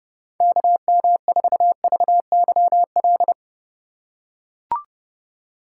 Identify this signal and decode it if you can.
Morse code, heard as a keyed tone, about 30 wpm, 700 Hz. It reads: KM4VYL
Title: USA Call Signs - 6 Characters - 2x Character Spacing - 30wpm